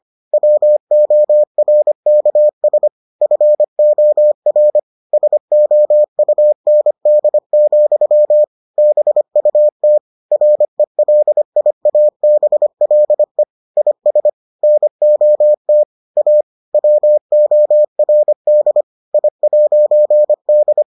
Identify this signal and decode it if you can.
Morse code, heard as a keyed tone, about 25 wpm, 600 Hz. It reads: WORKS FOR SOUND, BUT RELIABLE IS NOT A WORD I'D